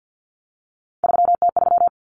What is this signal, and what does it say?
Morse code, heard as a keyed tone, about 50 wpm, 700 Hz. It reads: 3T3
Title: Common Cut Numbers - 3-Digit 50wpm